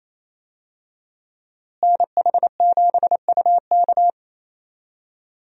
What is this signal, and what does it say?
Morse code, heard as a keyed tone, about 28 wpm, 700 Hz. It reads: NH7UK
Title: USA Call Signs - 5 Characters - Repeated 2x - 28wpm